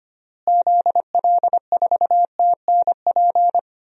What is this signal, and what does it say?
Morse code, heard as a keyed tone, about 25 wpm, 700 Hz. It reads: ZL4TNP
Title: DXCC-100 Call Signs - 6 Characters - Repeated 3x - 25wpm